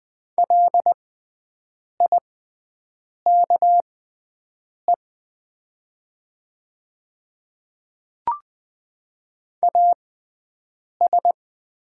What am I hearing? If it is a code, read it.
Morse code, heard as a keyed tone, about 20 wpm, spaced as for 7 wpm, 700 Hz. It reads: LIKE AS
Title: 6x Character Spacing - Top 200 Words - 20wpm